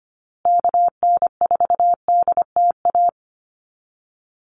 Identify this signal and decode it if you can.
Morse code, heard as a keyed tone, about 25 wpm, 700 Hz. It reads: KN4DTA
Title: USA Call Signs Repeated 2x 25wpm